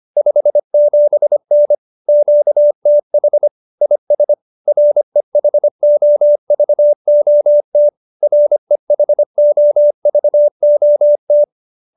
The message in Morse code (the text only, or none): 57N QTH IS REHOVOT REHOVOT